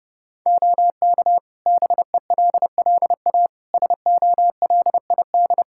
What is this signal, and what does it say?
Morse code, heard as a keyed tone, about 30 wpm, 700 Hz. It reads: OK BELLA SOLID
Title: K1USN SST QSO Practice 30wpm